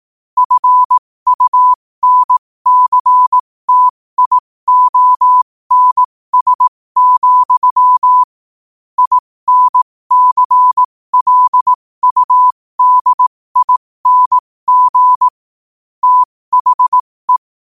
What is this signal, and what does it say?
Morse code, heard as a keyed tone, about 18 wpm, spaced as for 15 wpm, 1000 Hz. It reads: FUNCTIONS, INCLUDING THE
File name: qotdfiles / 15wpm